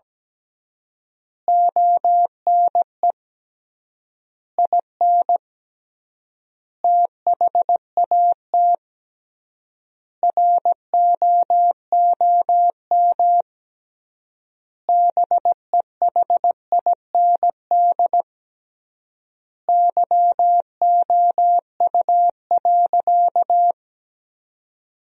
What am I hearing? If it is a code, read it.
Morse code, heard as a keyed tone, about 17 wpm, 700 Hz. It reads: ONE IN THAT ROOM BEHIND YOU.
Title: 3x Word-Spacing - Sentences from Top 500 Words - 17wpm